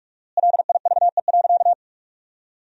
Morse code, heard as a keyed tone, about 41 wpm, 700 Hz. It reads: LIVE.
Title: 4x Word-Spacing - Sentences from Top 300 Words - 45wpm